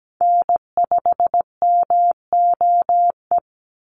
Morse code, heard as a keyed tone, about 17 wpm, 700 Hz. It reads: N5MOE
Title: USA Call Signs 17wpm